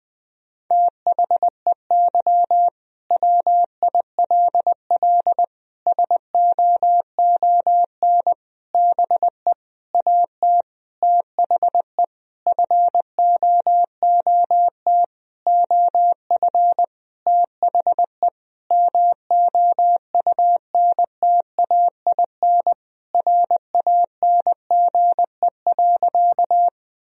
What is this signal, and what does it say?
Morse code, heard as a keyed tone, about 20 wpm, 700 Hz. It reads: THEY WILL SOON BE AT THE FOOT OF THE MOUNTAIN RANGE.
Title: Sentences from Top 1000 Words - Review Part 4 - 20wpm